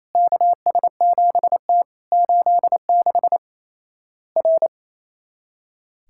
Morse code, heard as a keyed tone, about 28 wpm, 700 Hz. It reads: KS7T 86 R
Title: Worked All Europe DX Contest - QTC Exchange - 28wpm